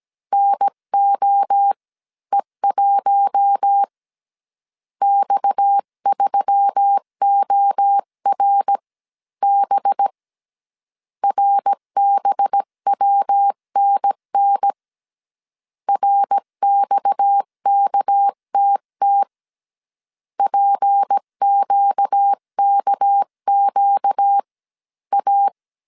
Morse code, heard as a keyed tone, about 17 wpm, spaced as for 14 wpm, 800 Hz. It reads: NOE1 X3ORB RBWNN RXKTT PQKQA
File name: CW / DET / 1 6 1500C S C 16wpm 2733